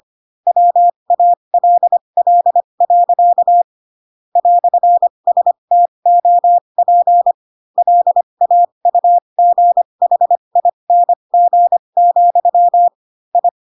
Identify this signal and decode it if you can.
Morse code, heard as a keyed tone, about 25 wpm, spaced as for 22 wpm, 700 Hz. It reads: WALL. "STOP LAUGHING, I
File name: Pinocchio / Pinocchio ch3 18 25 ik1hge